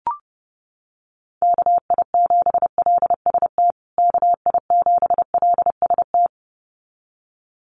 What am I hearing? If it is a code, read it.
Morse code, heard as a keyed tone, about 28 wpm, 700 Hz. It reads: KI7LST KI7LST
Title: USA Call Signs Repeated 2x 30wpm